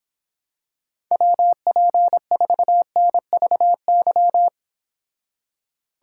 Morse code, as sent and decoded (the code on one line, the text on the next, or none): .-- .--. ....- -. ...- -.--
WP4NVY